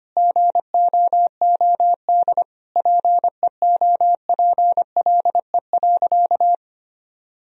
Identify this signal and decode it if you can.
Morse code, heard as a keyed tone, about 25 wpm, 700 Hz. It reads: GOOD PEOPLE.
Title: Sentences from Top 1000 Words - Review Part 2 - 25wpm